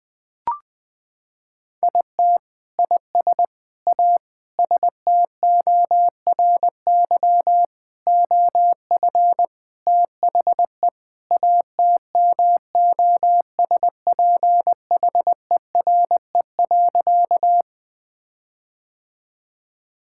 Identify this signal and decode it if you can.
Morse code, heard as a keyed tone, about 21 wpm, 700 Hz. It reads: IT IS A STORY OF THE ATMOSPHERE.
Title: Sentences from Top 1600 Words 20wpm